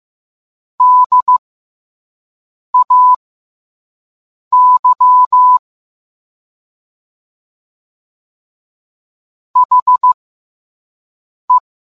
Morse code, heard as a keyed tone, about 15 wpm, spaced as for 5 wpm, 1000 Hz. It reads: DAY HE